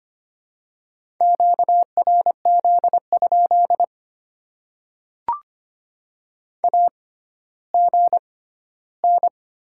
Morse code, heard as a keyed tone, about 25 wpm, spaced as for 14 wpm, 700 Hz. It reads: QRZ? AGN
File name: Top 90 QSO Elements - 6x Character Spacing - 25wpm